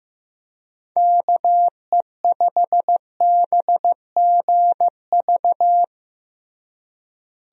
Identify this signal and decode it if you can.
Morse code, heard as a keyed tone, about 15 wpm, 700 Hz. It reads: KE5BGV
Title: USA Call Signs - 6 Characters - Repeated 2x - 15wpm